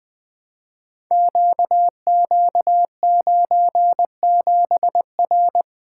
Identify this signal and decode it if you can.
Morse code, heard as a keyed tone, about 20 wpm, 700 Hz. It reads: QQ97R
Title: Code Groups - 5 Letter:Numbers - 20wpm